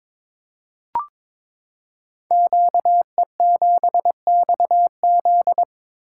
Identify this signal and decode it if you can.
Morse code, heard as a keyed tone, about 23 wpm, 700 Hz. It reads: QE7XZ
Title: Code Groups - 5 Letter:Numbers - 22wpm